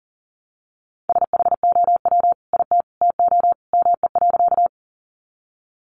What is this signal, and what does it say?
Morse code, heard as a keyed tone, about 39 wpm, 700 Hz. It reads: SHOW IT TO ME.
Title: Sentences from Top 200 Words 40wpm